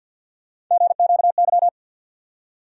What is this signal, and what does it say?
Morse code, heard as a keyed tone, about 53 wpm, 700 Hz. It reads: GQY
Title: USA Call Sign Postfixes - 3 Letters 50wpm